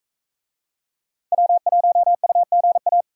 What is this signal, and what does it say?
Morse code, heard as a keyed tone, about 41 wpm, 700 Hz. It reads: W1UGA